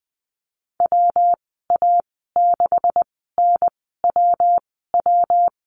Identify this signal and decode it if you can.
Morse code, heard as a keyed tone, about 20 wpm, spaced as for 15 wpm, 700 Hz. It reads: WA6NWW
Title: 193 Lesson 31g - Letter v - Call Signs 20wpm